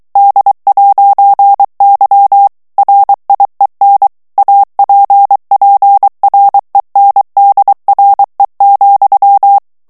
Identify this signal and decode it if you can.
Morse code, heard as a keyed tone, about 23 wpm, 800 Hz. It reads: D'Y RIEN APPRENDRE,
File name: morse mp3 / FRANCAIS TEXTE CLAIR 2 VITESSE 22 MOTS MINUTES